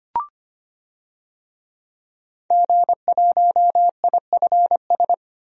Morse code, heard as a keyed tone, about 25 wpm, 700 Hz. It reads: G1IFS
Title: DXCC-100 Call Signs - 5 Characters - Repeated 3x - 25wpm